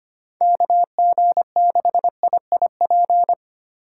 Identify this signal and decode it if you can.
Morse code, heard as a keyed tone, about 25 wpm, 700 Hz. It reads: KG6IIP
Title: K1USN SST QSO Practice - 25wpm